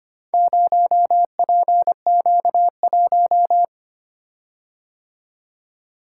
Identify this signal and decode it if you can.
Morse code, heard as a keyed tone, about 25 wpm, 700 Hz. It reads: 0PQ1
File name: practice / 25wpm / Four Letter-Number Combinations - 25wpm